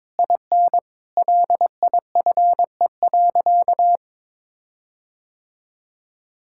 Morse code, heard as a keyed tone, about 22 wpm, 700 Hz. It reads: IN LIFE.
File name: Difficult Sentences with Hints - 22wpm